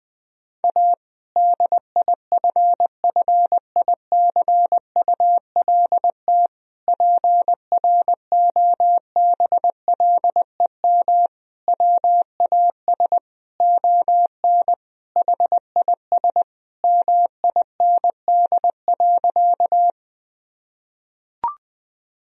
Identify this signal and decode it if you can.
Morse code, heard as a keyed tone, about 20 wpm, 700 Hz. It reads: A DIFFICULT PROBLEM WAS ON HIS MIND.
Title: Sentences from Top 1000 Words - Review Part 2 - 20wpm